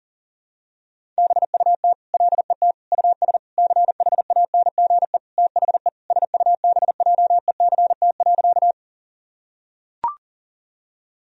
Morse code, heard as a keyed tone, about 40 wpm, 700 Hz. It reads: BUT LET US CHANGE THE SUBJECT.
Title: Sentences from Top 1000 Words - Review Part 1 - 40wpm